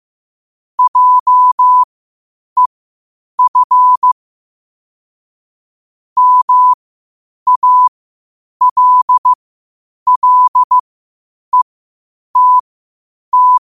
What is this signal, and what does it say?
Morse code, heard as a keyed tone, about 15 wpm, spaced as for 8 wpm, 1000 Hz. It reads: JEF MALLETT
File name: qotdfiles / 8wpm